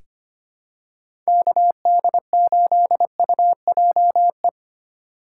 Morse code, heard as a keyed tone, about 25 wpm, 700 Hz. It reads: KD8UJE
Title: USA Call Signs - 3x Character Spacing - 25wpm